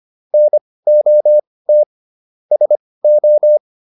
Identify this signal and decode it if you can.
Morse code, heard as a keyed tone, about 25 wpm, spaced as for 18 wpm, 600 Hz. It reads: NOT SO
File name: QSO408 18